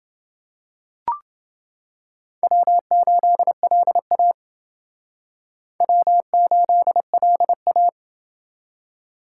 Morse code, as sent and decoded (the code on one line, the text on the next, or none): .-- ---.. .-.. .- / .-- ---.. .-.. .-
W8LA W8LA